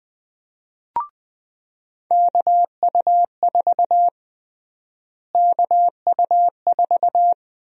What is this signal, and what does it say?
Morse code, heard as a keyed tone, about 21 wpm, 700 Hz. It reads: KU4 KU4
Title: USA Call Sign Prefixes - Repeated 2x - 20wpm